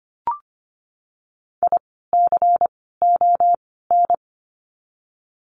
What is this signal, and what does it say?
Morse code, heard as a keyed tone, about 25 wpm, 700 Hz. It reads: I C O N
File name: course / 25:10wpm / 094 Lesson 16f - Letter c - Words 25wpm